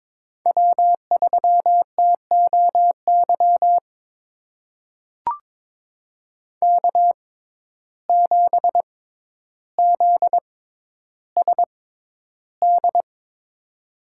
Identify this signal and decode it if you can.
Morse code, heard as a keyed tone, about 23 wpm, spaced as for 11 wpm, 700 Hz. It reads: W3TOY K7ZSD
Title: USA Call Signs - 5 Characters - 6x Character Spacing - 22wpm